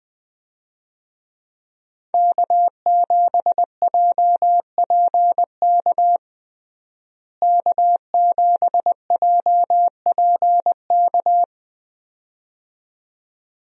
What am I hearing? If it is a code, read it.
Morse code, heard as a keyed tone, about 20 wpm, 700 Hz. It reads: K7JPK K7JPK
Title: USA Call Signs - 5 Characters - Repeated 2x - 20wpm